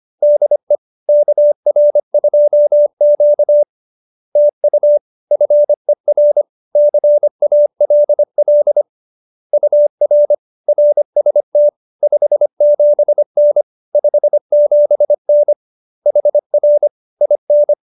Morse code, heard as a keyed tone, about 25 wpm, 590 Hz. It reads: DE KR2Q TU FER CALL UR RST 57N 57N HR IN